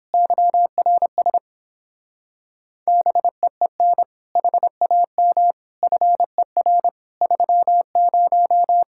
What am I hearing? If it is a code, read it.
Morse code, heard as a keyed tone, about 31 wpm, 700 Hz. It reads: YRS BEEN HAM FER 30